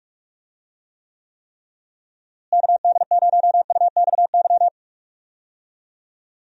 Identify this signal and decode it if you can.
Morse code, heard as a keyed tone, about 46 wpm, 700 Hz. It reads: KD0UXY
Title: USA Call Signs - 6 Characters - Repeated 3x - 45wpm